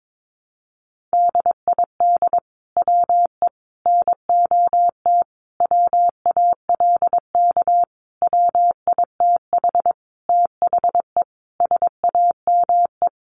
Morse code, heard as a keyed tone, about 22 wpm, 700 Hz. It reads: DID WE NOT WALK WITH THE SAME
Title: Sentences from Top 1400 Words 22wpm